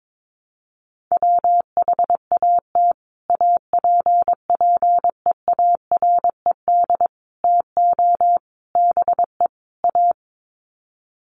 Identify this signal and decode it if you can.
Morse code, heard as a keyed tone, about 22 wpm, 700 Hz. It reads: WHAT APPEARED TO BE A
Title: Sets of 5 Words Encore2 22wpm